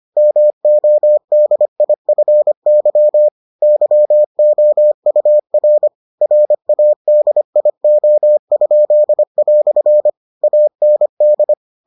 Morse code, heard as a keyed tone, about 25 wpm, 600 Hz. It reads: MODIFY YOUR RADIO?" AND